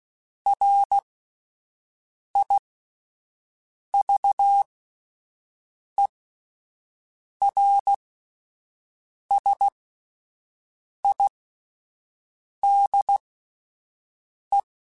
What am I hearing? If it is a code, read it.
Morse code, heard as a keyed tone, about 16 wpm, spaced as for 5 wpm, 800 Hz. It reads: RIVERSIDE